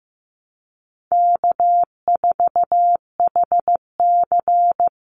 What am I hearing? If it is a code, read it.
Morse code, heard as a keyed tone, about 15 wpm, 700 Hz. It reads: K4HC
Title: Easy Call Signs Repeated 2x - 15wpm